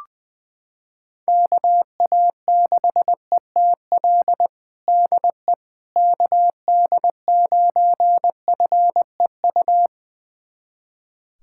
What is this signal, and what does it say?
Morse code, heard as a keyed tone, about 20 wpm, 700 Hz. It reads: KA6ETL DE KD9FEU
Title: USA Callsign DE Callsign - 6 Characters - 20wpm